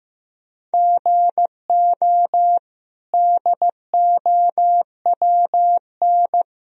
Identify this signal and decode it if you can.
Morse code, heard as a keyed tone, about 16 wpm, 700 Hz. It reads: GO DOWN
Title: Sentences from Top 1000 Words - Review Part 4 - 15wpm